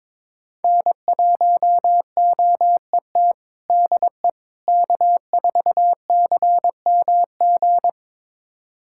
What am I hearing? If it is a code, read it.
Morse code, heard as a keyed tone, about 22 wpm, 700 Hz. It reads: N1OET DE K4CMG
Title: USA Callsign DE Callsign - 5 Characters - 22wpm